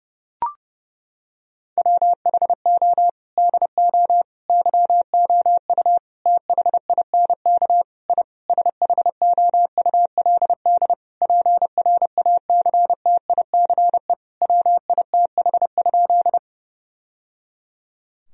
Morse code, heard as a keyed tone, about 32 wpm, 700 Hz. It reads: WHO DO YOU THINK I SHOULD PRACTICE WITH?
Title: Sentences from Top 1000 Words - Review Part 4 - 30wpm